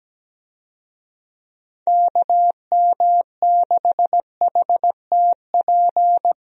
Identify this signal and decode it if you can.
Morse code, heard as a keyed tone, about 17 wpm, 700 Hz. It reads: KM6HTP
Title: USA Call Signs - 6 Characters - Repeated 2x - 17wpm